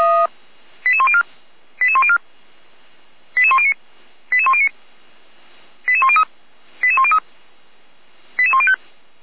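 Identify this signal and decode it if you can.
Five-tone selective call (ZVEI), heard as quick runs of five tones, tones of about 70 ms. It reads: 88104 88104 88109 88109 88103 88103 88105
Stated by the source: ZVEI five-tone selective call, code 89241, received via SDR, fire brigade Orlamünde (feuerwehr-orlamuende).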